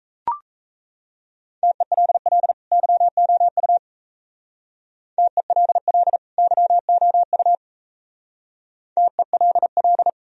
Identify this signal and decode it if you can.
Morse code, heard as a keyed tone, about 35 wpm, 700 Hz. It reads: TELL YOU TELL YOU TELL
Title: Speed-Racing - Sets of 2 Words 28wpm